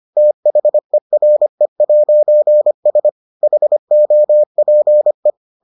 Morse code, heard as a keyed tone, about 25 wpm, 600 Hz. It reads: THERE'S HOPE